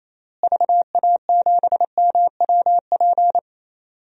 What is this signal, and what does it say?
Morse code, heard as a keyed tone, about 28 wpm, 700 Hz. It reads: VA7MWP
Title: DXCC-100 Call Signs - 6 Characters - Repeated 2x - 28wpm